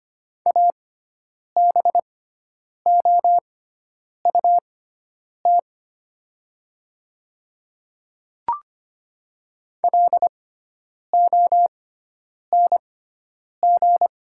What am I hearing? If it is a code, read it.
Morse code, heard as a keyed tone, about 25 wpm, spaced as for 9 wpm, 700 Hz. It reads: ABOUT LONG
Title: 6x Character Spacing - Top 100 Words - 25wpm